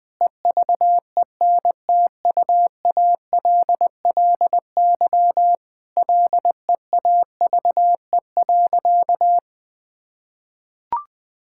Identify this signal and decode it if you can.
Morse code, heard as a keyed tone, about 20 wpm, 700 Hz. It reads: EVENTUALLY LEAVE.
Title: Sentences from Top 1000 Words - Review Part 6 - 20wpm